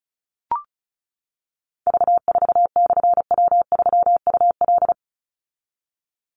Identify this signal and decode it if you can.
Morse code, heard as a keyed tone, about 34 wpm, 700 Hz. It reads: V4/W3UL
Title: Call Signs - 35wpm